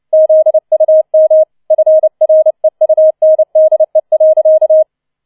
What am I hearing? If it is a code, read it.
Morse code, heard as a keyed tone, about 29 wpm, 630 Hz. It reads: ZUM FREUNDE.